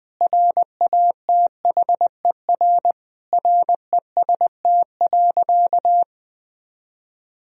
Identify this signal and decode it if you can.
Morse code, heard as a keyed tone, about 20 wpm, 700 Hz. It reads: RATHER REST.
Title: Sentences from Top 1000 Words - Review Part 4 - 20wpm